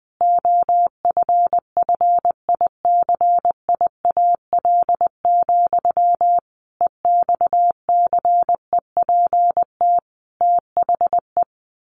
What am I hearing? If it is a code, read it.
Morse code, heard as a keyed tone, about 19 wpm, 700 Hz. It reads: OFFICIAL, EXCEPT THE
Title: A Sound Of Thunder 20wpm